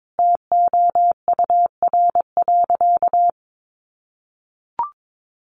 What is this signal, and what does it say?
Morse code, heard as a keyed tone, about 22 wpm, 700 Hz. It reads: TOUR.
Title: Sentences from Top 1400 Words 22wpm